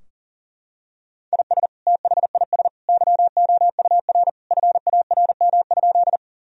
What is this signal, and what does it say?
Morse code, heard as a keyed tone, about 40 wpm, 700 Hz. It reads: IS THIS YOUR FARM?
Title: Sentences from Top 1000 Words - Review Part 2 - 40wpm